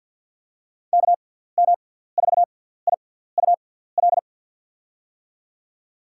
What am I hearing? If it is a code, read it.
Morse code, heard as a keyed tone, about 52 wpm, spaced as for 17 wpm, 700 Hz. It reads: KM4IUL
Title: USA Call Signs - 6 Characters - 6x Character Spacing - 50wpm